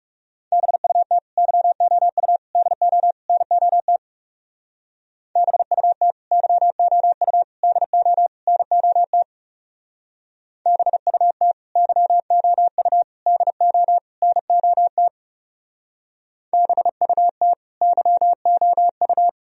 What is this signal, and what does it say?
Morse code, heard as a keyed tone, about 37 wpm, 700 Hz. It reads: BUT YOU DO NOT BUT YOU DO NOT BUT YOU DO NOT BUT YOU
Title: Speed-Racing - Sets of 4 Words 30wpm